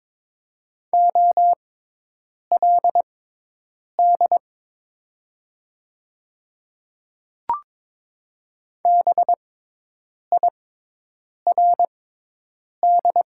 Morse code, heard as a keyed tone, about 22 wpm, spaced as for 8 wpm, 700 Hz. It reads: OLD BIRD
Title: 6x Character Spacing - Top 500 Words - 22wpm